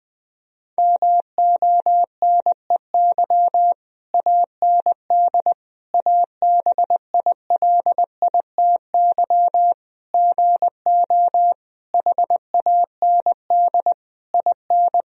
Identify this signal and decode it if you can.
Morse code, heard as a keyed tone, about 20 wpm, 700 Hz. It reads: MONEY AND ABILITY GO HAND IN
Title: Sentences from Top 1000 Words - Review Part 3 - 20wpm